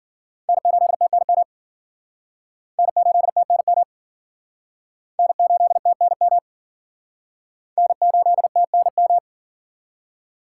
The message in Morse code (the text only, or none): N8TNM N8TNM N8TNM N8TNM